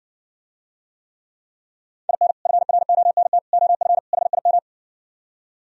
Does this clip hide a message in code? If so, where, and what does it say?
Yes: Morse code from 2.1 to 4.6 s, heard as a keyed tone, about 69 wpm, 700 Hz. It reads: IN FRONT OF HIM